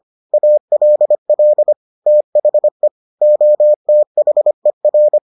Morse code, heard as a keyed tone, about 25 wpm, 600 Hz. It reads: ALL THE OTHER